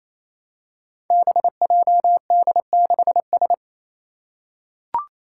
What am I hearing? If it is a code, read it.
Morse code, heard as a keyed tone, about 27 wpm, 700 Hz. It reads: BJD6S
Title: Five Challenging Letter-Number Combinations 28wpm